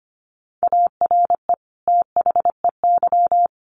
Morse code, heard as a keyed tone, about 25 wpm, 700 Hz. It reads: ARE THEY
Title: Sentences from Top 1300 Words 25wpm